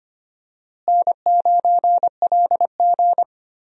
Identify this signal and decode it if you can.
Morse code, heard as a keyed tone, about 25 wpm, 700 Hz. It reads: N9LG
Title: USA Call Signs - 4 Characters - Repeated 2x - 25wpm